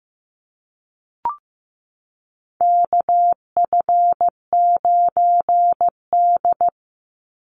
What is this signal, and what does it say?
Morse code, heard as a keyed tone, about 15 wpm, 700 Hz. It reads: KF9D
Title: Call Signs Repeated 2x - 15wpm